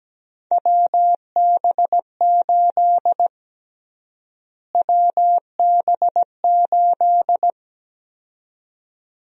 Morse code, heard as a keyed tone, about 17 wpm, 700 Hz. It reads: WB8 WB8
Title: USA Call Sign Prefixes - Repeated 2x - 17wpm